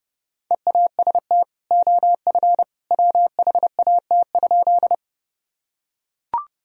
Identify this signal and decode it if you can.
Morse code, heard as a keyed tone, about 30 wpm, 700 Hz. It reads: EAST OF WHAT?
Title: Sentences from Top 1000 Words - Review Part 3 - 30wpm